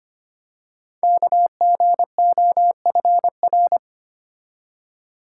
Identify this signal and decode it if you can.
Morse code, heard as a keyed tone, about 25 wpm, 700 Hz. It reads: KGOFR
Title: Code Groups - 5 Letters - 25wpm